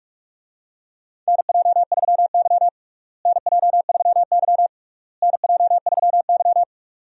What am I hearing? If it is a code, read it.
Morse code, heard as a keyed tone, about 46 wpm, 700 Hz. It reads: NJ3Y NJ3Y NJ3Y
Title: USA Call Signs - 4 Characters - Repeated 3x - 45wpm